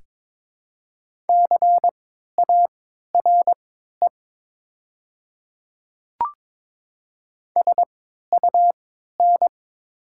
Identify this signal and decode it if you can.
Morse code, heard as a keyed tone, about 22 wpm, spaced as for 12 wpm, 690 Hz. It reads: CARE SUN
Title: 3x Character Spacing - Top 300 Words - 22wpm